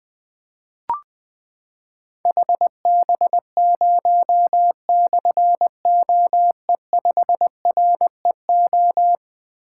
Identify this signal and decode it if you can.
Morse code, heard as a keyed tone, about 20 wpm, 700 Hz. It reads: HB0/OE5REO
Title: Difficult Call Signs 20wpm